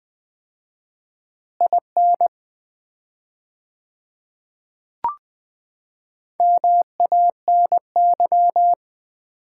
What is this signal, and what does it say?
Morse code, heard as a keyed tone, about 20 wpm, 700 Hz. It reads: IN MANY
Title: 4x Word-Spacing - Sets of 3 Words - 20wpm